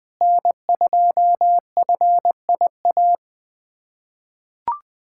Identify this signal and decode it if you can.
Morse code, heard as a keyed tone, about 20 wpm, 700 Hz. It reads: N2FIA
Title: USA Callsign DE Callsign - 5 Characters - 20wpm